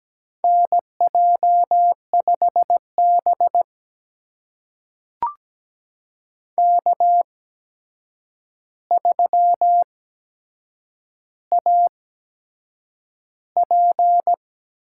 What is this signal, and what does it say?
Morse code, heard as a keyed tone, about 17 wpm, spaced as for 7 wpm, 700 Hz. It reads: NJ5B K3AP
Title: USA Call Signs - 8x Character Spacing - 17wpm